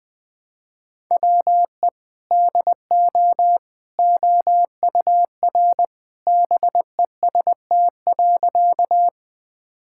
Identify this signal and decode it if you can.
Morse code, heard as a keyed tone, about 20 wpm, 700 Hz. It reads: WE DO OUR BEST.
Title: Sentences with 2-Word Phrases - Spoken Beginning - 20wpm